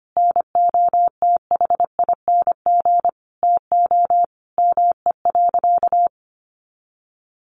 Morse code, heard as a keyed tone, about 25 wpm, 700 Hz. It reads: NOTHING TO ME.
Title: Sentences from Top 1400 Words 25wpm